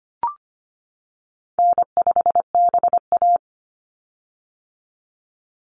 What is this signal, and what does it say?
Morse code, heard as a keyed tone, about 25 wpm, 700 Hz. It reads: N5BA
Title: Easy Call Signs - 25wpm